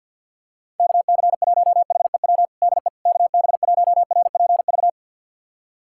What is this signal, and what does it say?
Morse code, heard as a keyed tone, about 52 wpm, 700 Hz. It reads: KC1HEW DE KB1RPV
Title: USA Callsign DE Callsign - 6 Characters - 50wpm